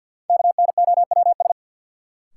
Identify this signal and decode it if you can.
Morse code, heard as a keyed tone, about 46 wpm, 700 Hz. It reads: KNOWS
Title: Sentences from Top 1000 Words - Review Part 1 - New Word Drill - 50wpm